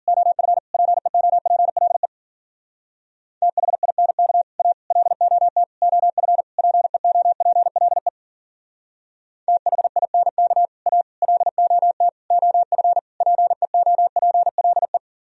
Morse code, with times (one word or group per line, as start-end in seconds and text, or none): 0.1-0.6 OF
0.7-2.1 PEOPLE
3.4-4.4 THINK
4.6-4.7 A
4.9-5.6 LOT
5.8-6.4 OF
6.6-8.1 PEOPLE
9.5-10.7 THINK
10.9-11.0 A
11.2-12.1 LOT
12.3-13.0 OF
13.2-15.0 PEOPLE